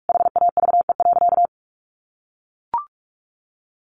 Morse code, heard as a keyed tone, about 44 wpm, 700 Hz. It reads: HAVE.
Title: Sentences from Top 1100 Words 45wpm